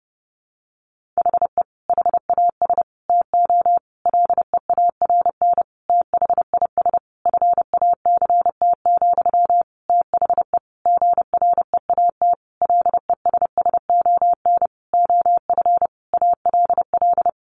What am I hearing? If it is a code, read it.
Morse code, heard as a keyed tone, about 30 wpm, 700 Hz. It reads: HE HAS TO LEARN THIS FACT, THE GREAT LESSON OF ALL
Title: Sentences from Top 1200 Words 30wpm